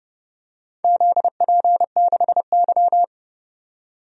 Morse code, heard as a keyed tone, about 31 wpm, 700 Hz. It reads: ZP6Y
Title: DXCC-100 Call Signs - 4 Characters - 30wpm